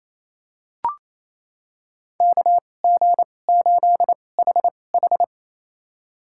Morse code, heard as a keyed tone, about 28 wpm, spaced as for 20 wpm, 700 Hz. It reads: KG8HH
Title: USA Call Signs - 2x Character Spacing - 28wpm